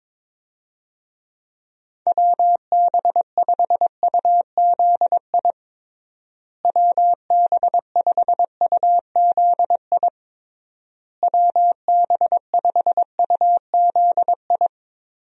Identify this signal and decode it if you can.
Morse code, heard as a keyed tone, about 22 wpm, 700 Hz. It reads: WB5UZI WB5UZI WB5UZI
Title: USA Call Signs - 6 Characters - Repeated 3x - 22wpm